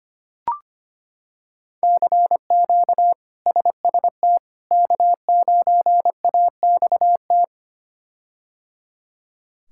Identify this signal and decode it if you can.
Morse code, heard as a keyed tone, about 25 wpm, 700 Hz. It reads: CQ SST K9AXT
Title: K1USN SST QSO Practice - 25wpm